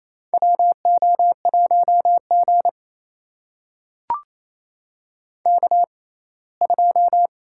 Morse code, heard as a keyed tone, about 28 wpm, spaced as for 19 wpm, 700 Hz. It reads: WO1G K2
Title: USA Call Signs - 6x Character Spacing - 28wpm